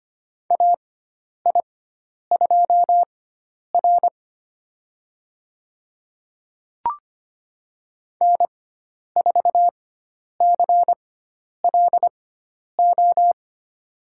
Morse code, heard as a keyed tone, about 25 wpm, spaced as for 10 wpm, 700 Hz. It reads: AI2R N4CLO
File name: Lesson 16g - Letter c - Call Signs 25wpm